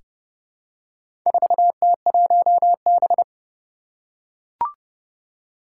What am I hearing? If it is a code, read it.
Morse code, heard as a keyed tone, about 32 wpm, 700 Hz. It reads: 4T1B
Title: Four Challenging Letter-Number Combinations 30wpm